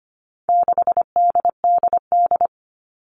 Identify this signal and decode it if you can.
Morse code, heard as a keyed tone, about 25 wpm, 700 Hz. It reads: 6DDD
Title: All Cut Numbers - 4-Digit 25wpm